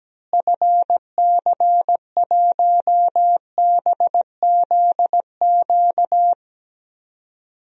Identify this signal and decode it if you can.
Morse code, heard as a keyed tone, about 17 wpm, 700 Hz. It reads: FC1BZQ
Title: DXCC-100 Call Signs - 6 Characters - Repeated 2x - 17wpm